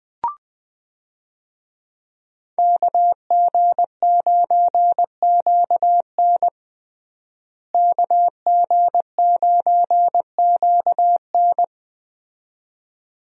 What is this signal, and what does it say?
Morse code, heard as a keyed tone, about 20 wpm, 700 Hz. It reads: KG9QN KG9QN
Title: USA Call Signs - 5 Characters - Repeated 2x - 20wpm